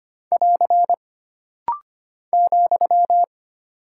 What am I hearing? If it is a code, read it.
Morse code, heard as a keyed tone, about 25 wpm, 700 Hz. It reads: <AR> ,
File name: Single Punctuation-Prosign - Rapid-Fire - 25wpm